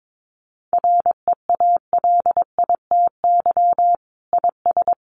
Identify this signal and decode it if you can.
Morse code, heard as a keyed tone, about 22 wpm, 700 Hz. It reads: REALITY IS
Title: Sentences from Top 1200 Words 22wpm